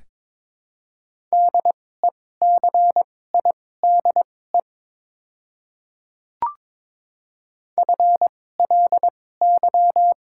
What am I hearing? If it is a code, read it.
Morse code, heard as a keyed tone, about 22 wpm, spaced as for 16 wpm, 700 Hz. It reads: DECIDE FLY
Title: 2x Character Spacing - Top 500 Words - 22wpm